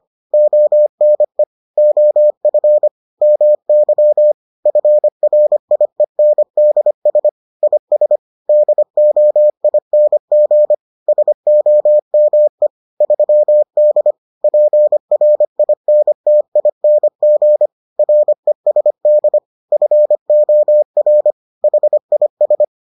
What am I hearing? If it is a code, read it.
Morse code, heard as a keyed tone, about 25 wpm, 600 Hz. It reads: ONE OF MY FRIENDS IS DOING SOME 3D PRINTING RESD FOR HIS